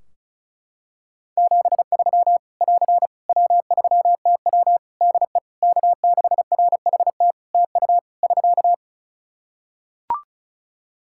Morse code, heard as a keyed tone, about 36 wpm, 700 Hz. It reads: 73 <AR> W3TW DE K6RHT TU <SK>
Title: Practice QSO - Exchange 4 - Ending - 35wpm